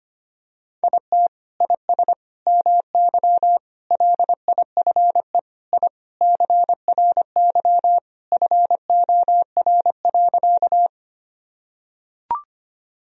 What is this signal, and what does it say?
Morse code, heard as a keyed tone, about 25 wpm, 700 Hz. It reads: IT IS MY LIFE I CRY FOR.
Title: Sentences from Top 1000 Words - Review Part 2 - 25wpm